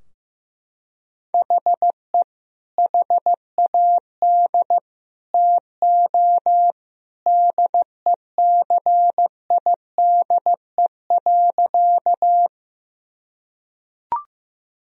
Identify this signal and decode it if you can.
Morse code, heard as a keyed tone, about 15 wpm, 700 Hz. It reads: HE HAD TO DECIDE.
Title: Sentences from Top 1000 Words - Review Part 3 - 15wpm